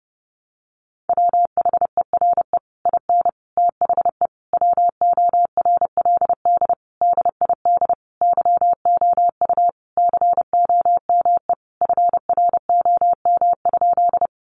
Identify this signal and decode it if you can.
Morse code, heard as a keyed tone, about 30 wpm, 700 Hz. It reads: WHERE IN THE WORLD DID YOU COME FROM?
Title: Sentences from Top 200 Words 30wpm